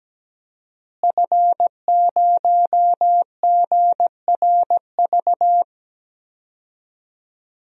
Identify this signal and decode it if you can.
Morse code, heard as a keyed tone, about 17 wpm, 700 Hz. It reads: F0GRV
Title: DXCC-100 Call Signs - 5 Characters - Repeated 2x - 17wpm